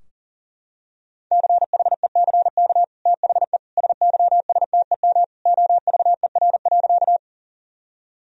Morse code, heard as a keyed tone, about 37 wpm, 700 Hz. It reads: CHECK THE SYSTEM OVER.
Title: Sentences from Top 1000 Words - Review Part 3 - 40wpm